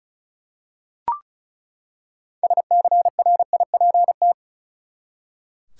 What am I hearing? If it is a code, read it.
Morse code, heard as a keyed tone, about 34 wpm, 700 Hz. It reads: SCRIPT
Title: Six Letter Words - 35wpm